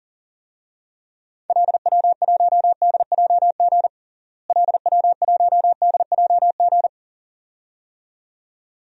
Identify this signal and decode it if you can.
Morse code, heard as a keyed tone, about 37 wpm, 700 Hz. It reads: LW1DJG LW1DJG
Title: DXCC-100 Call Signs - 6 Characters - Repeated 2x - 40wpm